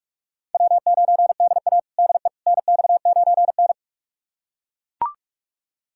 Morse code, heard as a keyed tone, about 42 wpm, 700 Hz. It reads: W9DA DE NX9N
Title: USA Callsign DE Callsign - 4 Characters - 45wpm